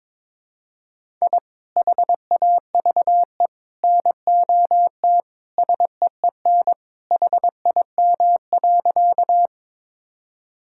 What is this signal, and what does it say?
Morse code, heard as a keyed tone, about 22 wpm, 700 Hz. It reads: I HAVE NOT SEEN HIM.
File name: practice / 22wpm / Sentences with 3-Word Phrases - 22wpm